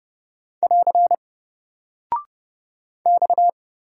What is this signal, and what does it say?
Morse code, heard as a keyed tone, about 30 wpm, 700 Hz. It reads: <AR> X
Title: Single Character (Letter:Number:Punctuation:Prosign) Alt 30wpm